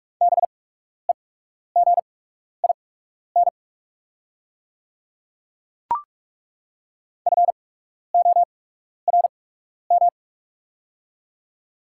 Morse code, heard as a keyed tone, about 41 wpm, spaced as for 12 wpm, 700 Hz. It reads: BEGIN FORM
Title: 8x Character Spacing - Top 300 Words - 45wpm